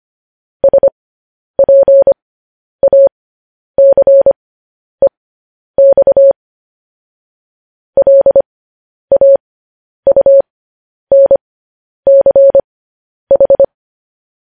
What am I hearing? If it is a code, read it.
Morse code, heard as a keyed tone, about 25 wpm, spaced as for 10 wpm, 550 Hz. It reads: SPACEX LAUNCH
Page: news-headlines-in-morse-code-at-10-wpm